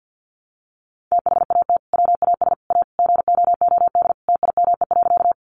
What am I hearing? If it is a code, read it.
Morse code, heard as a keyed tone, about 50 wpm, 700 Hz. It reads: THAT WAS A GOOD TIME.
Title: Sentences from Top 200 Words 50wpm